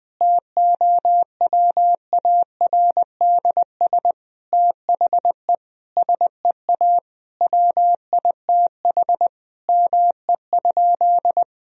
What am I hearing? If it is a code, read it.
Morse code, heard as a keyed tone, about 20 wpm, 700 Hz. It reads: TOWARDS THE SEA WITH ME?
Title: Sentences from Top 1000 Words - Review Part 2 - 20wpm